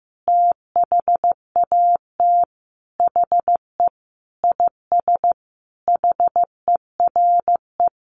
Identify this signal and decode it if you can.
Morse code, heard as a keyed tone, about 15 wpm, 700 Hz. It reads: THAT HE IS HERE
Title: Sentences from Top 1100 Words 15wpm